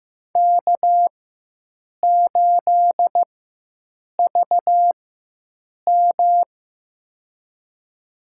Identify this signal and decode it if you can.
Morse code, heard as a keyed tone, about 15 wpm, spaced as for 7 wpm, 700 Hz. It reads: K8VM
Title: USA Call Signs - 4 Characters - 4x Character Spacing - 15wpm